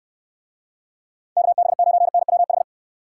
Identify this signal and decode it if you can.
Morse code, heard as a keyed tone, about 73 wpm, 700 Hz. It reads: KB9NWH